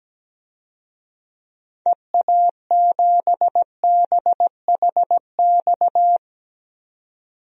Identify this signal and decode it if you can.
Morse code, heard as a keyed tone, about 17 wpm, 700 Hz. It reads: EA7BHX
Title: DXCC-100 Call Signs - 6 Characters - 17wpm